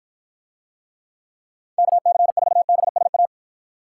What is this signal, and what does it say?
Morse code, heard as a keyed tone, about 52 wpm, 700 Hz. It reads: KC4BSA